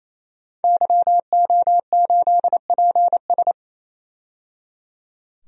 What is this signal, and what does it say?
Morse code, heard as a keyed tone, about 28 wpm, 700 Hz. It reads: YO8PS
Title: DXCC-100 Call Signs - 5 Characters - 28wpm